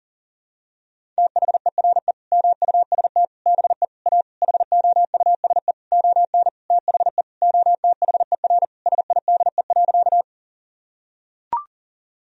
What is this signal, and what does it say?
Morse code, heard as a keyed tone, about 40 wpm, 700 Hz. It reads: THERE MUST BE A HOUSE ON THE OTHER SIDE.
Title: Sentences from Top 1000 Words - Review Part 1 - 40wpm